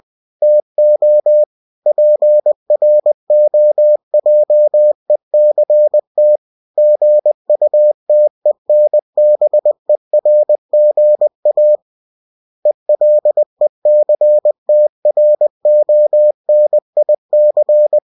Morse code, heard as a keyed tone, about 20 wpm, 600 Hz. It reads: TO PROJECT GUTENBERGA ELECTRONIC
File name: cw-books / winnie-the-pooh / winnie-20wpm-page-0113